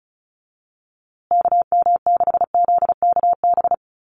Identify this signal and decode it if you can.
Morse code, heard as a keyed tone, about 34 wpm, 700 Hz. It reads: KM6ZKB
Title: USA Call Signs Repeated 2x 35wpm